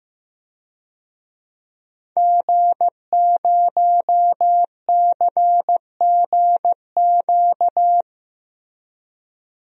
Morse code, heard as a keyed tone, about 15 wpm, 700 Hz. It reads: G0CGQ
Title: DXCC-100 Call Signs - 5 Characters - Repeated 3x - 15wpm